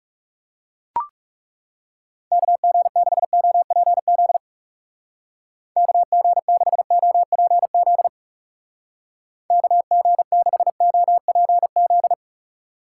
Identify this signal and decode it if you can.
Morse code, heard as a keyed tone, about 39 wpm, 700 Hz. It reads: KG6OPZ KG6OPZ KG6OPZ